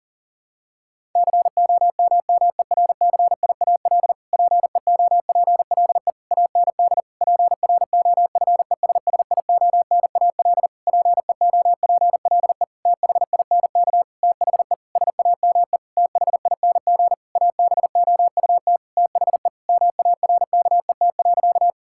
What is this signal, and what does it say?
Morse code, heard as a keyed tone, about 40 wpm, 700 Hz. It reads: COMMERCIAL PEOPLE AND PROFESSIONAL PEOPLE THINK THE SAME THING ABOUT THE MARKET.
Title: Sentences from Top 1000 Words - Review Part 5 - 40wpm